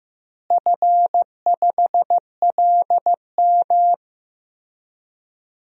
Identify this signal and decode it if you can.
Morse code, heard as a keyed tone, about 15 wpm, 700 Hz. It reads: F5LM
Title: DXCC-100 Call Signs - 4 Characters - Repeated 3x - 15wpm